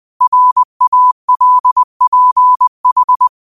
Morse code, heard as a keyed tone, about 20 wpm, 1000 Hz. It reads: RALPH
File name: qotdfiles / 20wpm